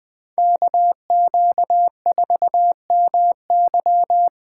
Morse code, heard as a keyed tone, about 20 wpm, 700 Hz. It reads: KQ4MY
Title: Speed Racing - USA 2x2 Call Signs 17wpm